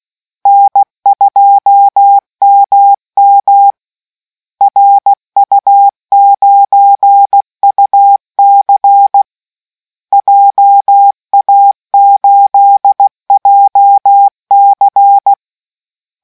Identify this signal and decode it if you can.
Morse code, heard as a keyed tone, about 16 wpm, 800 Hz. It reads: N2MM RU9UC JA8JC